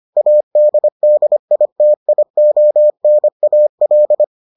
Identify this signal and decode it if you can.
Morse code, heard as a keyed tone, about 25 wpm, 600 Hz. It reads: ADDITIONAL